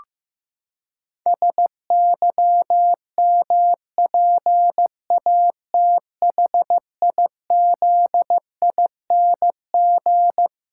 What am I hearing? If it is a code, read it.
Morse code, heard as a keyed tone, about 15 wpm, 700 Hz. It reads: SYMPATHIZING
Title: Pangram Challenge v2 - 15wpm